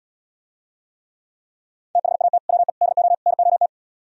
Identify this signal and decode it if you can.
Morse code, heard as a keyed tone, about 77 wpm, 700 Hz. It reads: THAT WE DO NOT